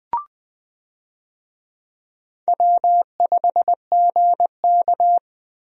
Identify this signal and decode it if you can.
Morse code, heard as a keyed tone, about 20 wpm, 700 Hz. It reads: W5GK
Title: USA Call Signs - 4 Characters - Repeated 3x - 20wpm